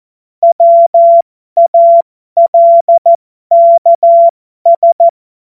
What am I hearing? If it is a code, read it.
Morse code, heard as a keyed tone, about 14 wpm, spaced as for 12 wpm, 680 Hz. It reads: WALKS
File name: Morsekurs / Witze